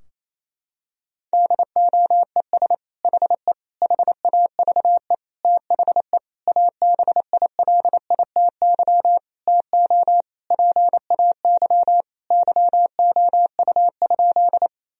Morse code, heard as a keyed tone, about 28 wpm, 700 Hz. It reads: DOES HE HAVE THE ABILITY TO PAY YOU?
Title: Sentences from Top 1000 Words - Review Part 3 - 28wpm